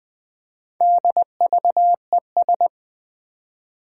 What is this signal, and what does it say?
Morse code, heard as a keyed tone, about 20 wpm, 700 Hz. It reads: DVES
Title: Four Challenging Letter-Number Combinations 20wpm